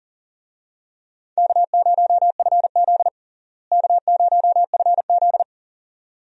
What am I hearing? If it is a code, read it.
Morse code, heard as a keyed tone, about 40 wpm, 700 Hz. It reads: K0FZ K0FZ
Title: USA Call Signs - 4 Characters - Repeated 2x - 40wpm